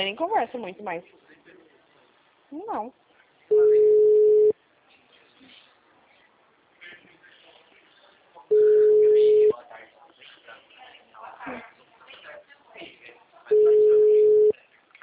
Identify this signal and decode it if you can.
Ringback tone (Europe, 425 Hz), with speech.